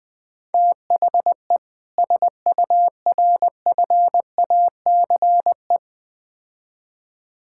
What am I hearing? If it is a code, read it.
Morse code, heard as a keyed tone, about 21 wpm, 700 Hz. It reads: THE SURFACE
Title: Build A Phrase - 4-Word Phrases - 20wpm